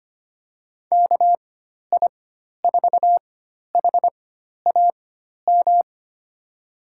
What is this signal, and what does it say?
Morse code, heard as a keyed tone, about 25 wpm, spaced as for 12 wpm, 700 Hz. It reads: KI4HAM
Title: USA Call Signs - 6 Characters - 4x Character Spacing - 25wpm